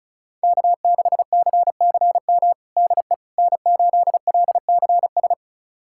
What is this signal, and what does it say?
Morse code, heard as a keyed tone, about 36 wpm, 700 Hz. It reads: K6CCM DE N8LCS
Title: USA Callsign DE Callsign - 5 Characters - 35wpm